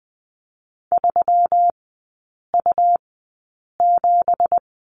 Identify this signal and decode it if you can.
Morse code, heard as a keyed tone, about 20 wpm, spaced as for 8 wpm, 700 Hz. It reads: 3U7
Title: 178 Lesson 29e - Slash - Three Characters 20wpm